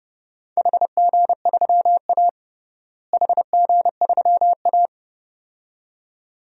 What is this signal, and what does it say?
Morse code, heard as a keyed tone, about 31 wpm, 700 Hz. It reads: HG3A HG3A
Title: DXCC-100 Call Signs - 4 Characters - Repeated 2x - 30wpm